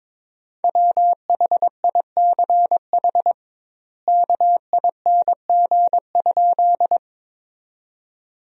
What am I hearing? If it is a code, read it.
Morse code, heard as a keyed tone, about 22 wpm, 700 Hz. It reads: WHICH KING?
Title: 2x Word-Spacing - Sentences from Top 400 Words - 22wpm